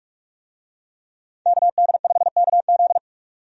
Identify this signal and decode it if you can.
Morse code, heard as a keyed tone, about 42 wpm, 700 Hz. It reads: KD5KZ
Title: USA Call Signs - 5 Characters - 45wpm